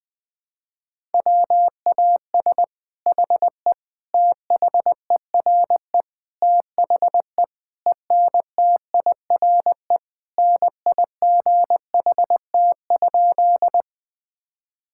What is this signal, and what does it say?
Morse code, heard as a keyed tone, about 21 wpm, 700 Hz. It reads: WAS HE THERE THE ENTIRE NIGHT?
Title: Sentences from Top 1000 Words - Review Part 5 - 20wpm